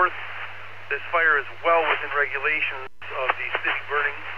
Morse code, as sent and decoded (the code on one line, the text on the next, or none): none